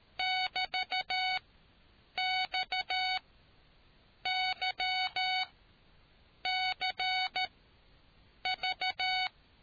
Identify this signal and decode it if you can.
Morse code, heard as a keyed tone, about 13 wpm, spaced as for 7 wpm, 750 Hz. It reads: <BT>XYCV